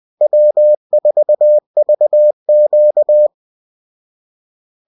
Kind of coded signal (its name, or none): Morse code